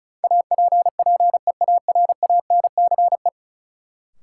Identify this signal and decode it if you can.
Morse code, heard as a keyed tone, about 34 wpm, 700 Hz. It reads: APPEARANCE
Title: Sentences from Top 1100 Words - New Word Drill Alt - 35wpm